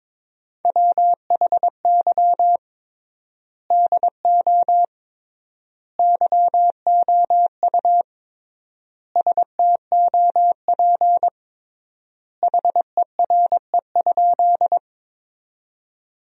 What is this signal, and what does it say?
Morse code, heard as a keyed tone, about 21 wpm, 700 Hz. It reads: WHY DO YOU STOP HERE?
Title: 3x Word-Spacing - Sentences from Top 300 Words - 22wpm